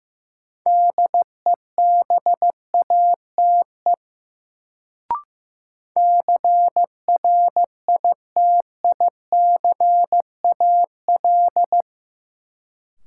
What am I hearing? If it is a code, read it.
Morse code, heard as a keyed tone, about 15 wpm, 700 Hz. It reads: DEBATE CRITICAL
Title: Sentences from Top 1100 Words - New Word Drill - 15wpm